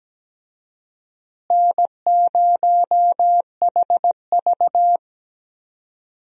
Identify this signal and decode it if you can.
Morse code, heard as a keyed tone, about 17 wpm, 700 Hz. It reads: N0HV
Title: USA Call Signs - 4 Characters - Repeated 3x - 17wpm